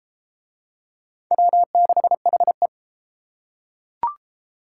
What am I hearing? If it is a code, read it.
Morse code, heard as a keyed tone, about 30 wpm, 700 Hz. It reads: W6HE